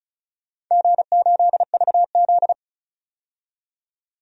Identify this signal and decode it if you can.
Morse code, heard as a keyed tone, about 36 wpm, 700 Hz. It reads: G8VZ
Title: DXCC-100 Call Signs - 4 Characters - 35wpm